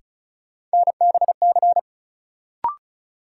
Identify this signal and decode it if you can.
Morse code, heard as a keyed tone, about 36 wpm, 700 Hz. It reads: NBC
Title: Sets of 5 Words Encore - New Word Drill - 35wpm